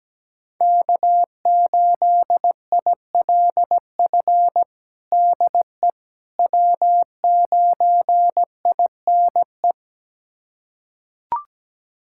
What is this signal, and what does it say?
Morse code, heard as a keyed tone, about 17 wpm, 700 Hz. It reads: K8ILF DE W9INE
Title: USA Callsign DE Callsign - 5 Characters - 17wpm